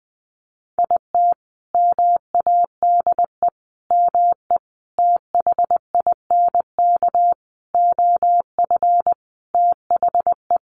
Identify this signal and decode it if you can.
Morse code, heard as a keyed tone, about 20 wpm, 700 Hz. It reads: IT MADE ME THINK OF THE
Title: Sentences from Top 1100 Words 20wpm